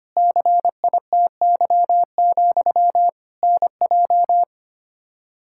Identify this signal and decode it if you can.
Morse code, heard as a keyed tone, about 25 wpm, 700 Hz. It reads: CITY, NJ
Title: K1USN SST QSO Practice 25wpm